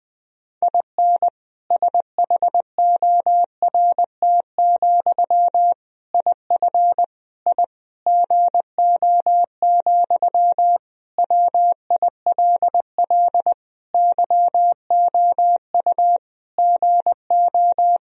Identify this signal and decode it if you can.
Morse code, heard as a keyed tone, about 20 wpm, 700 Hz. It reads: IN SHORT, IF I GO, WILL YOU GO
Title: Sentences from Top 1000 Words - Review Part 2 - 20wpm